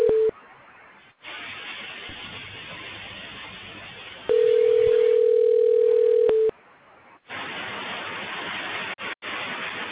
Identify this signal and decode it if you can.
Ringback tone (North America), 440 + 480 Hz.